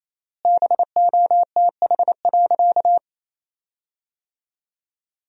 Morse code, heard as a keyed tone, about 28 wpm, 700 Hz. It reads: BOTH.
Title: 3x Word-Spacing - Sentences from Top 400 Words - 28wpm